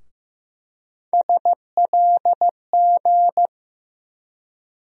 Morse code, heard as a keyed tone, about 15 wpm, 700 Hz. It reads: SLG
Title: USA Call Sign Postfixes - 3 Letters 15wpm